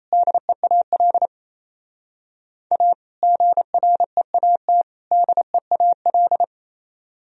Morse code, heard as a keyed tone, about 34 wpm, 700 Hz. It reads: DEAL A GREAT DEAL
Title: Speed-Racing - Sets of 3 Words 28wpm